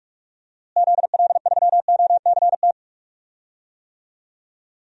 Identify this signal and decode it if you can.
Morse code, heard as a keyed tone, about 43 wpm, 700 Hz. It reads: ZL3OCT
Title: DXCC-100 Call Signs - 6 Characters - Repeated 3x - 45wpm